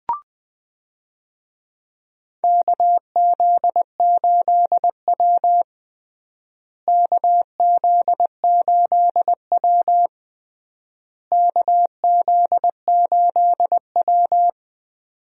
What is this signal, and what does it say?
Morse code, heard as a keyed tone, about 20 wpm, 700 Hz. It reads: KZ8W KZ8W KZ8W
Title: USA Call Signs - 4 Characters - Repeated 3x - 20wpm